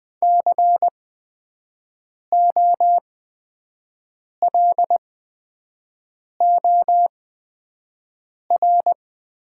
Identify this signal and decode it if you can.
Morse code, heard as a keyed tone, about 20 wpm, spaced as for 5 wpm, 700 Hz. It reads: COLOR
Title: 8x Character Spacing - Top 500 Words - 20wpm